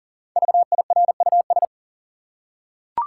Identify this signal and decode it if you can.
Morse code, heard as a keyed tone, about 40 wpm, 700 Hz. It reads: VIRUS
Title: Five Letter Words - 40wpm